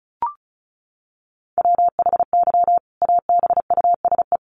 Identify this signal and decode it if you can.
Morse code, heard as a keyed tone, about 34 wpm, 700 Hz. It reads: WHY ABUSE
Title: Sentences from Top 1300 Words 35wpm